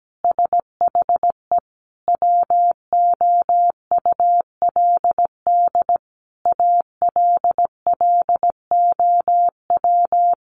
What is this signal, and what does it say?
Morse code, heard as a keyed tone, about 17 wpm, 700 Hz. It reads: SHE WOULD ALLOW
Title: Sentences from Top 700 Words 17wpm